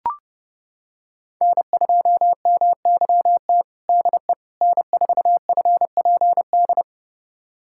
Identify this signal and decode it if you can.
Morse code, heard as a keyed tone, about 30 wpm, 700 Hz. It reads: N2MYT DE N4FPD
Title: USA Callsign DE Callsign - 5 Characters - 30wpm